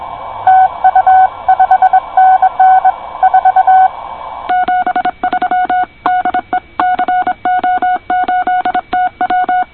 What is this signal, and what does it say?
Morse code, heard as a keyed tone, about 25 wpm, 720 Hz. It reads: TU 5NN 4 73 DE CO8TW
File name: co8tw / cqwwcw2010